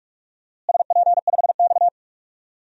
Morse code, heard as a keyed tone, about 46 wpm, 700 Hz. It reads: SP5X
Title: DXCC-100 Call Signs - 4 Characters - Repeated 2x - 45wpm